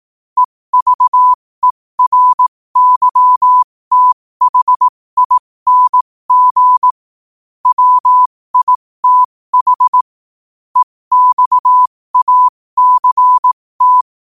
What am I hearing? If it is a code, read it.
Morse code, heard as a keyed tone, about 18 wpm, spaced as for 15 wpm, 1000 Hz. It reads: EVERYTHING WITH EXACT
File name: qotdfiles / 15wpm